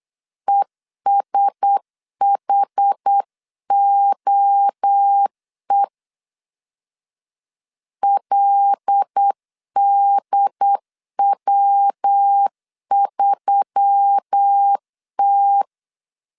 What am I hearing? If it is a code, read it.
Morse code, heard as a keyed tone, about 8 wpm, 800 Hz. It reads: ESHOE LDW3T
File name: 8wpm